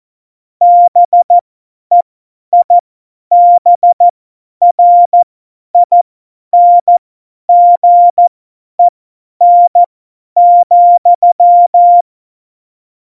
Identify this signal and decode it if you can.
Morse code, heard as a keyed tone, about 14 wpm, spaced as for 10 wpm, 700 Hz. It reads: BEIBRINGEN,
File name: Morsekurs / Witze / Witz 23. April 2019, 50 F70